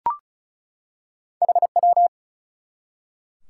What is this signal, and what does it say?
Morse code, heard as a keyed tone, about 36 wpm, 700 Hz. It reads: HW
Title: Top 180 QSO Elements 35wpm